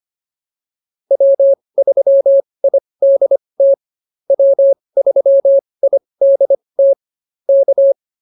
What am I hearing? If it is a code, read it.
Morse code, heard as a keyed tone, about 25 wpm, spaced as for 20 wpm, 550 Hz. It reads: W3IDT W3IDT K